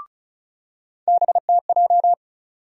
Morse code, heard as a keyed tone, about 36 wpm, 700 Hz. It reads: BTJ
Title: USA Call Sign Postfixes - 3 Letters 35wpm